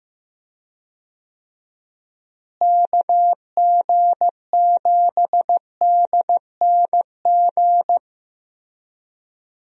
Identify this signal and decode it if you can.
Morse code, heard as a keyed tone, about 15 wpm, 700 Hz. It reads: KG7DNG
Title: USA Call Signs - 6 Characters - Repeated 2x - 15wpm